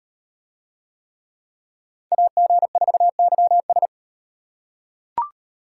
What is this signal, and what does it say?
Morse code, heard as a keyed tone, about 37 wpm, 700 Hz. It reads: AG4YS